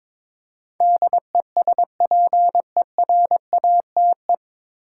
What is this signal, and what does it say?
Morse code, heard as a keyed tone, about 22 wpm, 700 Hz. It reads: DESPERATE
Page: Morse Code Ninja practice 22wpm